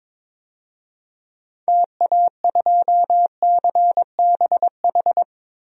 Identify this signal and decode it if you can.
Morse code, heard as a keyed tone, about 22 wpm, 700 Hz. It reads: TA2CBH
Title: DXCC-100 Call Signs - 6 Characters - Repeated 2x - 22wpm